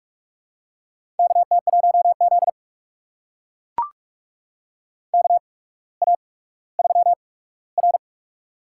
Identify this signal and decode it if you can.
Morse code, heard as a keyed tone, about 44 wpm, spaced as for 19 wpm, 690 Hz. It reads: KT1Z KA3R
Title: USA Call Signs - 4 Characters - 8x Character Spacing - 45wpm